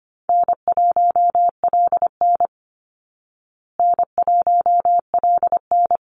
Morse code, heard as a keyed tone, about 25 wpm, 700 Hz. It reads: N1LN N1LN
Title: Easy Call Signs Repeated 2x - 25wpm